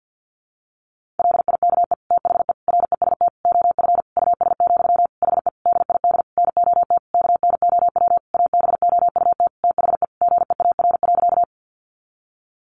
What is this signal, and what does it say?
Morse code, heard as a keyed tone, about 50 wpm, 700 Hz. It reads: LIKE THE REST OF US, HE DID NOT KNOW ABOUT THE GEAR.
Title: Sentences from Top 1400 Words 50wpm